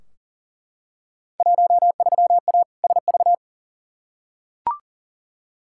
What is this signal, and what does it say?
Morse code, heard as a keyed tone, about 40 wpm, 700 Hz. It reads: 13A SV
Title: Field Day Exchanges Encore - 40wpm